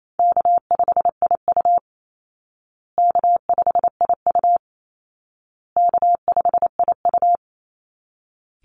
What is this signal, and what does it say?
Morse code, heard as a keyed tone, about 28 wpm, 700 Hz. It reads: K5IU K5IU K5IU
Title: Easy Call Signs Repeated 3x - 28wpm